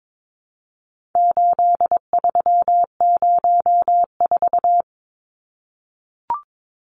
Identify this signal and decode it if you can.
Morse code, heard as a keyed tone, about 22 wpm, 700 Hz. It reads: 8304
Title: Numbers - 4-Digit 22wpm